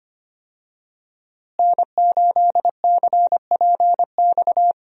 Morse code, heard as a keyed tone, about 25 wpm, 700 Hz. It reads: N8CPX
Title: USA Call Signs - 5 Characters - Repeated 3x - 25wpm